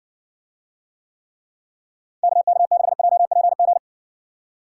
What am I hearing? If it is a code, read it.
Morse code, heard as a keyed tone, about 60 wpm, 700 Hz. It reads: KK6JPG